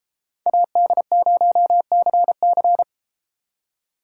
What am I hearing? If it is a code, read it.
Morse code, heard as a keyed tone, about 33 wpm, 700 Hz. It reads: AD0CC